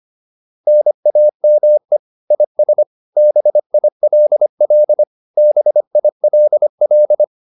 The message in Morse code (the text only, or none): NAME IS BILL BILL